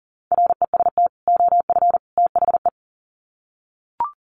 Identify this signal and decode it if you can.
Morse code, heard as a keyed tone, about 40 wpm, 700 Hz. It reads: REST OF THE
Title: Sets of 3 Words 40wpm